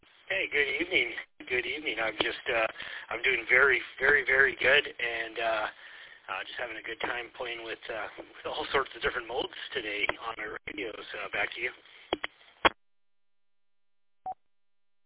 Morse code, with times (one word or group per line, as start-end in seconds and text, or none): none